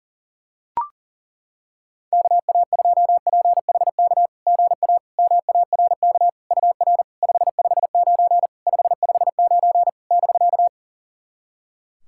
Morse code, heard as a keyed tone, about 37 wpm, 690 Hz. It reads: KA2PHK GA MARK UR 559 559 <BK>
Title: SOTA QSO Practice 40wpm